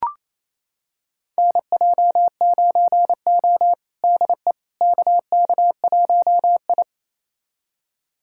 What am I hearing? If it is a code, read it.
Morse code, heard as a keyed tone, about 28 wpm, 700 Hz. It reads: NJ9O DE KK1I
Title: USA Callsign DE Callsign - 4 Characters - 28wpm